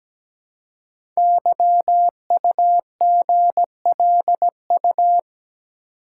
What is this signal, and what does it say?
Morse code, heard as a keyed tone, about 17 wpm, 700 Hz. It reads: YUGLU
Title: Code Groups - 5 Letter:Numbers - 17wpm